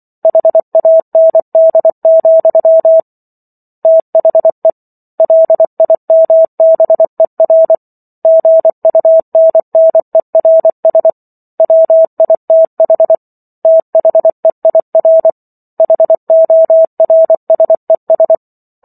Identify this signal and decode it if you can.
Morse code, heard as a keyed tone, about 24 wpm, 650 Hz. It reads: HAND, THE LIMBER GUNNERS WITH THEIR HORSES